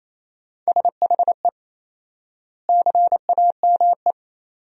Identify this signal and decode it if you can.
Morse code, heard as a keyed tone, about 27 wpm, 700 Hz. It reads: SHE CAME
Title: 4x Word-Spacing - Sentences from Top 300 Words - 28wpm